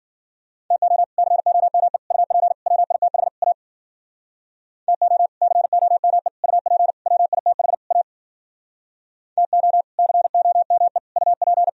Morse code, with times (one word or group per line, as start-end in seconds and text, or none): 0.7-1.0 TO
1.2-2.0 COME
2.1-2.5 UP
2.7-3.3 WITH
3.4-3.5 A
4.9-5.3 TO
5.4-6.3 COME
6.4-6.9 UP
7.1-7.7 WITH
7.9-8.0 A
9.4-9.8 TO
10.0-11.0 COME
11.2-11.7 UP